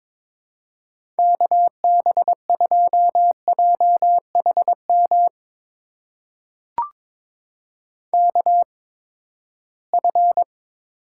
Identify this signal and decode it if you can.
Morse code, heard as a keyed tone, about 22 wpm, spaced as for 15 wpm, 700 Hz. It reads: KB2JHM KF
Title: USA Call Signs - 6 Characters - 8x Character Spacing - 22wpm